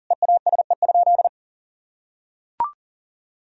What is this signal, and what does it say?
Morse code, heard as a keyed tone, about 37 wpm, 700 Hz. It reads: EASE?
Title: Sentences from Top 1000 Words - Review Part 1 - 40wpm